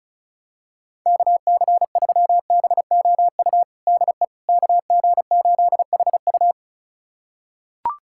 Morse code, heard as a keyed tone, about 34 wpm, 700 Hz. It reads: KC3BOU DE KG8HU
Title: USA Callsign DE Callsign 35wpm